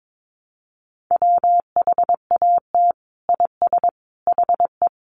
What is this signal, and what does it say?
Morse code, heard as a keyed tone, about 22 wpm, 700 Hz. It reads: WHAT IS HE
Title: Sentences from Top 100 Words 22wpm